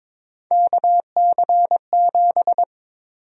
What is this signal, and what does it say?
Morse code, heard as a keyed tone, about 21 wpm, 700 Hz. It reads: KC7
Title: USA Call Sign Prefixes - Repeated 2x - 22wpm